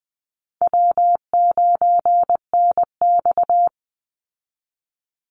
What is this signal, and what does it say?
Morse code, heard as a keyed tone, about 20 wpm, 700 Hz. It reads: W9NX
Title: Easy Call Signs Repeated 2x - 20wpm